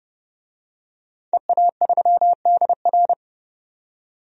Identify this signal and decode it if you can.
Morse code, heard as a keyed tone, about 30 wpm, 700 Hz. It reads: EA3DR
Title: DXCC-100 Call Signs - 5 Characters - 30wpm